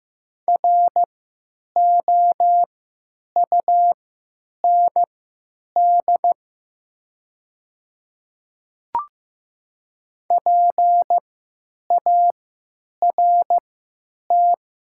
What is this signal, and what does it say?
Morse code, heard as a keyed tone, about 15 wpm, spaced as for 9 wpm, 700 Hz. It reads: ROUND PART
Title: 3x Character Spacing - Top 200 Words - 15wpm